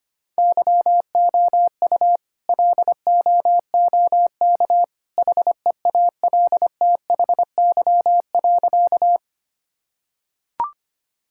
Morse code, heard as a keyed tone, about 25 wpm, 700 Hz. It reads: YOU LOOK HEALTHY.
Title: Sentences from Top 1000 Words - Review Part 5 - 25wpm